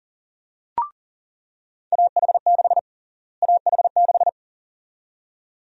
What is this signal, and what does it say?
Morse code, heard as a keyed tone, about 40 wpm, 700 Hz. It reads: AH6 AH6
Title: USA Call Sign Prefixes - Repeated 2x - 40wpm